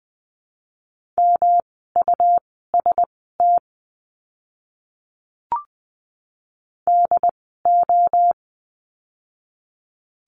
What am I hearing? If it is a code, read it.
Morse code, heard as a keyed tone, about 20 wpm, spaced as for 14 wpm, 700 Hz. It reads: MUST DO
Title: 237 Lesson 38f - Letter q - Words 20wpm